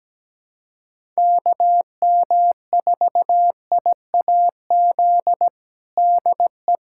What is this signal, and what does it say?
Morse code, heard as a keyed tone, about 17 wpm, 700 Hz. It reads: KM4IAZ DE
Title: USA Callsign DE Callsign 17wpm